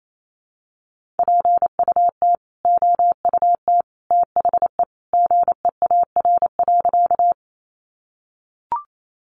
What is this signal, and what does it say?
Morse code, heard as a keyed tone, about 28 wpm, 700 Hz. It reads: PUT OUT THE GEAR.
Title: Sentences from Top 1400 Words 28wpm